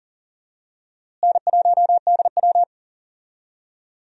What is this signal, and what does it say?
Morse code, heard as a keyed tone, about 40 wpm, 700 Hz. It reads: N1DW
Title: USA Call Signs - 4 Characters - Repeated 2x - 40wpm